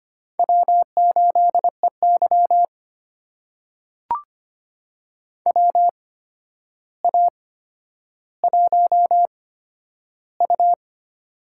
Morse code, heard as a keyed tone, about 25 wpm, spaced as for 11 wpm, 700 Hz. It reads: W8EY WA1U
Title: USA Call Signs - 4 Characters - 8x Character Spacing - 25wpm